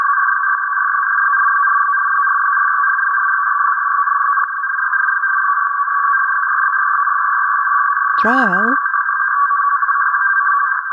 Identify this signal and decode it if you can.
Two-tone teleprinter signal, about 450 Hz shift, with speech.
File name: ilg free-archive / utility-telex-pinneberg-7646khz